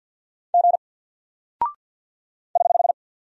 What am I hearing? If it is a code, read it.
Morse code, heard as a keyed tone, about 48 wpm, 700 Hz. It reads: G <HH>
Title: Single Character (Letter:Number:Punctuation:Prosign) Alt 50wpm